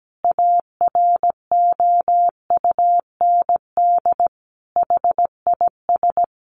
Morse code, heard as a keyed tone, about 17 wpm, 700 Hz. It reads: AROUND HIS
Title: A Sound Of Thunder 17wpm